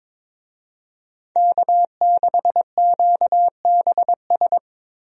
Morse code, heard as a keyed tone, about 22 wpm, 700 Hz. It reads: K6QBS
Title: USA Call Signs - 5 Characters - Repeated 2x - 22wpm